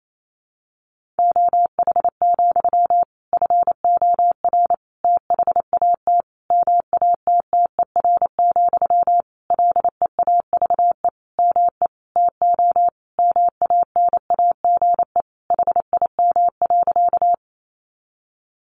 Morse code, heard as a keyed tone, about 28 wpm, 700 Hz. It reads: OH, FOR THAT MATTER, LEAVE ME TO MANAGE HIM.
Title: Sentences from Top 1400 Words 28wpm